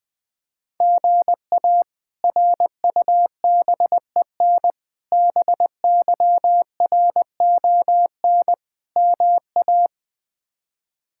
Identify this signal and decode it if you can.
Morse code, heard as a keyed tone, about 21 wpm, 700 Hz. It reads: GA RUBEN BYRON MA
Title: K1USN SST QSO Practice - 20wpm